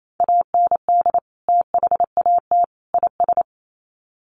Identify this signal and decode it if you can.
Morse code, heard as a keyed tone, about 28 wpm, 700 Hz. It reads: AND THAT IS
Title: Sets of 3 Words 28wpm